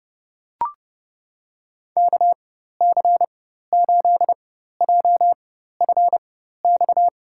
Morse code, heard as a keyed tone, about 30 wpm, spaced as for 14 wpm, 700 Hz. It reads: KC8JFX
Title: USA Call Signs - 6 Characters - 4x Character Spacing - 30wpm